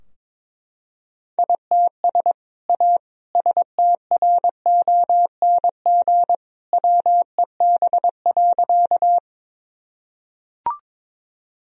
Morse code, heard as a keyed tone, about 22 wpm, 700 Hz. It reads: ITS A STRONG WEB.
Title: Sentences from Top 1000 Words - Review Part 4 - 22wpm